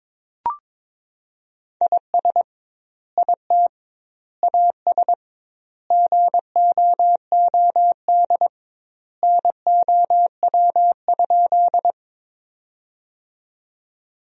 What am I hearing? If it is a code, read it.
Morse code, heard as a keyed tone, about 22 wpm, 700 Hz. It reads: IS IT AS GOOD NOW?
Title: 2x Word-Spacing - Sentences from Top 200 Words - 22wpm